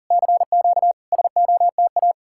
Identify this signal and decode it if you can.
Morse code, heard as a keyed tone, about 52 wpm, 700 Hz. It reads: CQ SOTA